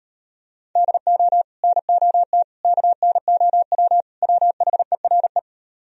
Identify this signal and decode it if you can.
Morse code, heard as a keyed tone, about 33 wpm, 700 Hz. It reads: DO NOT KNOW WHERE